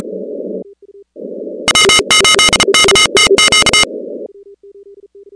Morse code, heard as a keyed tone, about 33 wpm, 2950 Hz, with noise. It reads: W8KTQ
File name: contests / 2019-cw-wpx-kt5j-k5pi- -2330 0-15